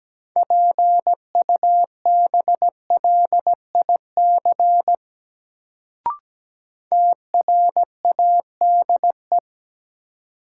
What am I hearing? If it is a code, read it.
Morse code, heard as a keyed tone, about 17 wpm, 700 Hz. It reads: PUBLIC TRADE
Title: Sentences from Top 600 Words - New Word Drill - 17wpm